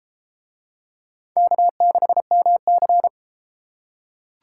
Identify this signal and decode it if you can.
Morse code, heard as a keyed tone, about 33 wpm, 700 Hz. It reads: K6MC